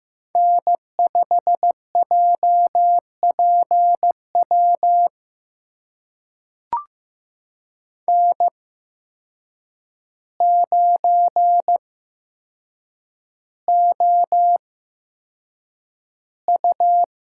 Morse code, heard as a keyed tone, about 15 wpm, spaced as for 7 wpm, 700 Hz. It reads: N5JPW N9OU
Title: USA Call Signs - 5 Characters - 8x Character Spacing - 15wpm